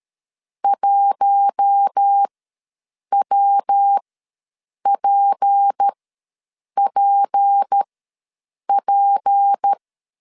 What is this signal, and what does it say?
Morse code, heard as a keyed tone, about 13 wpm, spaced as for 7 wpm, 800 Hz. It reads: 1WPPP
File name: LEZIONI / 6 PKX0 / 6 P ws 12wpm 3117